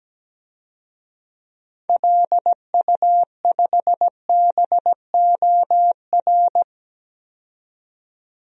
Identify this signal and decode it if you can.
Morse code, heard as a keyed tone, about 17 wpm, 700 Hz. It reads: LU5BOR
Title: DXCC-100 Call Signs - 6 Characters - 17wpm